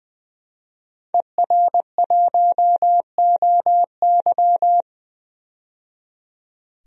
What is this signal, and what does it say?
Morse code, heard as a keyed tone, about 21 wpm, 700 Hz. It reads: ER1OY
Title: DXCC-100 Call Signs - 5 Characters - 20wpm